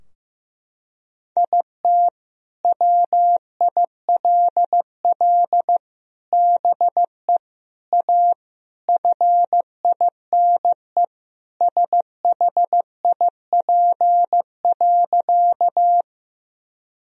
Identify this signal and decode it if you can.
Morse code, heard as a keyed tone, about 15 wpm, 700 Hz. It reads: IT WILL BE A FINE SHIP.
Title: Sentences from Top 1000 Words - Review Part 2 - 15wpm